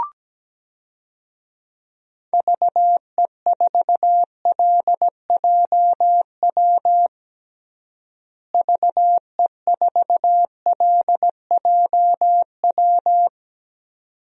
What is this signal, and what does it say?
Morse code, heard as a keyed tone, about 17 wpm, 700 Hz. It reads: VE4LJW VE4LJW
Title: DXCC-100 Call Signs - 6 Characters - Repeated 2x - 17wpm